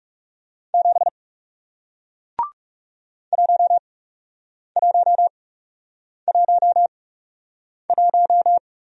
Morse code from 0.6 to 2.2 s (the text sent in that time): 7